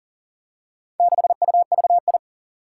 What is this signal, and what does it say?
Morse code, heard as a keyed tone, about 42 wpm, 700 Hz. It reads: 6UVI
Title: Four Challenging Letter-Number Combinations 40wpm